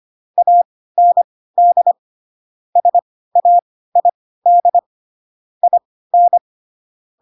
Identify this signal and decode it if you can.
Morse code, heard as a keyed tone, about 25 wpm, spaced as for 16 wpm, 700 Hz. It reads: AND SAID IN
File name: Pinocchio / Pinocchio ch3 10 25 ik1hge